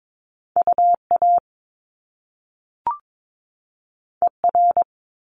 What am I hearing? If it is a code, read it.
Morse code, heard as a keyed tone, about 22 wpm, 700 Hz. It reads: UA ER
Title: Top Two Letter Combinations in Words 22wpm